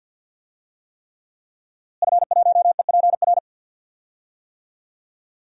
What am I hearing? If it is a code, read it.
Morse code, heard as a keyed tone, about 49 wpm, 700 Hz. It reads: F1EPR
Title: DXCC-100 Call Signs - 5 Characters - Repeated 3x - 50wpm